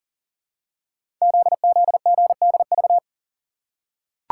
Morse code, heard as a keyed tone, about 38 wpm, 690 Hz. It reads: ZZGDV
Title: Code Groups - 5 Letters - 40wpm